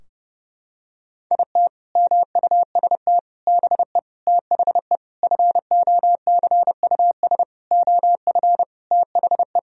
Morse code, heard as a keyed tone, about 30 wpm, 700 Hz. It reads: IT MUST BE THE FOCUS OF THE
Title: Sentences from Top 1000 Words - Review Part 4 - 30wpm